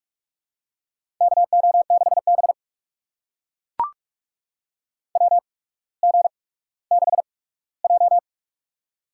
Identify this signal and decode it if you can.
Morse code, heard as a keyed tone, about 42 wpm, spaced as for 19 wpm, 700 Hz. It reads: KO6B WG6J
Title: USA Call Signs - 4 Characters - 8x Character Spacing - 45wpm